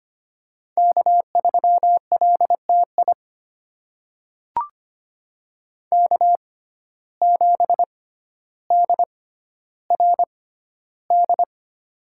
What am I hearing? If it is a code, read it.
Morse code, heard as a keyed tone, about 25 wpm, spaced as for 13 wpm, 700 Hz. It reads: K3LTI K7DRD
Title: USA Call Signs - 5 Characters - 6x Character Spacing - 25wpm